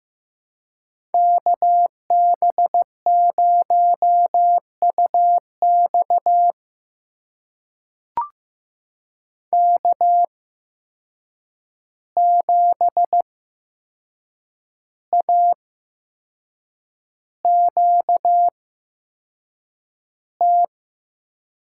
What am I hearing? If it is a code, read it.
Morse code, heard as a keyed tone, about 15 wpm, spaced as for 6 wpm, 700 Hz. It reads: KB0UX K7AQT
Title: USA Call Signs - 5 Characters - 8x Character Spacing - 15wpm